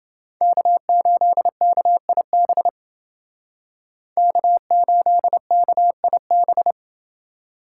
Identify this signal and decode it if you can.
Morse code, heard as a keyed tone, about 30 wpm, 700 Hz. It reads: K8KIB K8KIB